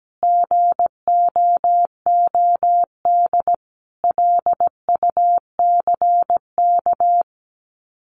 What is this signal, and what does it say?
Morse code, heard as a keyed tone, about 17 wpm, 700 Hz. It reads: GOOD LUCK
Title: Common QSO Phrases 17wpm